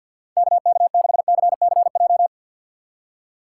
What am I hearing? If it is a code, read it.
Morse code, heard as a keyed tone, about 53 wpm, 700 Hz. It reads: KK6CCJ
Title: USA Call Signs - 6 Characters - 6x Character Spacing - 50wpm